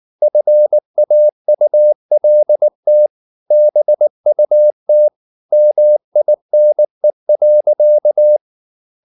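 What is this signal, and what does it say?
Morse code, heard as a keyed tone, about 19 wpm, 600 Hz. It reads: FAULT BUT MINE.
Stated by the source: Morse code at 19 wpm from CW med Gnister, reading Ken Follett's novel Never in English.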